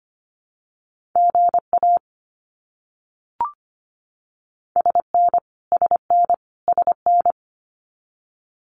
Morse code, heard as a keyed tone, about 25 wpm, 700 Hz. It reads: GA SN SN SN
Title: Abbreviations Repeated 3x - 25wpm